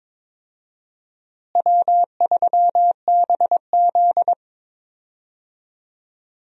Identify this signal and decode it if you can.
Morse code, heard as a keyed tone, about 22 wpm, 700 Hz. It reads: W3BZ
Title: USA Call Signs - 4 Characters - Repeated 3x - 22wpm